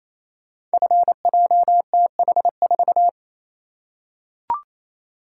Morse code, heard as a keyed tone, about 29 wpm, 700 Hz. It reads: FJTH4
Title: Code Groups - 5 Letter:Numbers - 28wpm